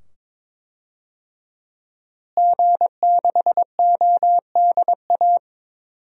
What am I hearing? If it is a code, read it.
Morse code, heard as a keyed tone, about 22 wpm, 700 Hz. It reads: G6ODA
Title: DXCC-100 Call Signs - 5 Characters - Repeated 2x - 22wpm